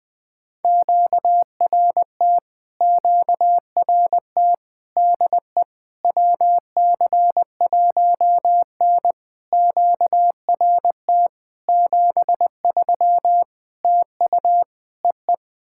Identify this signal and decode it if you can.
Morse code, heard as a keyed tone, about 21 wpm, 700 Hz. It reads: QRT QRT DE WC1N QRT 73 TU EE
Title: POTA QSO Practice - 20wpm